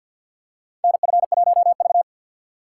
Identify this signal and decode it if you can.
Morse code, heard as a keyed tone, about 52 wpm, 700 Hz. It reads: NF1V
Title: USA Call Signs - 4 Characters - 3x Character Spacing - 50wpm